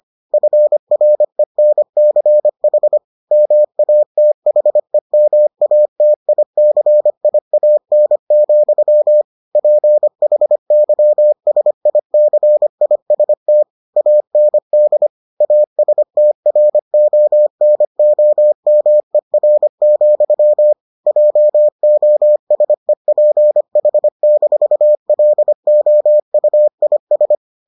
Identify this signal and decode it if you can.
Morse code, heard as a keyed tone, about 25 wpm, 600 Hz. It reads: FRENCH MATHEMATICIAN, PHYSICIST AND ASTRONOMER, JOSEPH-LOUIS